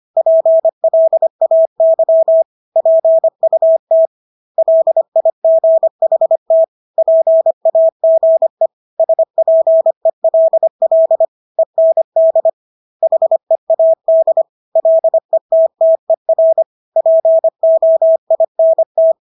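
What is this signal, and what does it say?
Morse code, heard as a keyed tone, about 25 wpm, 650 Hz. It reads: PLAY PUT LIGHT PAGE SPELL END HEAD LETTER POINT
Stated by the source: Morse practice of English words at 25 wpm from CW med Gnister.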